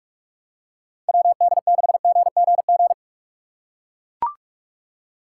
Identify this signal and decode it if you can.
Morse code, heard as a keyed tone, about 43 wpm, 700 Hz. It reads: WD6GGG
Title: USA Call Signs - 6 Characters - 3x Character Spacing - 45wpm